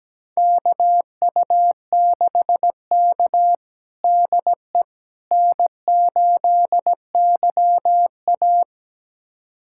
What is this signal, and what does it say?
Morse code, heard as a keyed tone, about 17 wpm, 700 Hz. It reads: KU6K DE N8YA
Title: USA Callsign DE Callsign - 4 Characters - 17wpm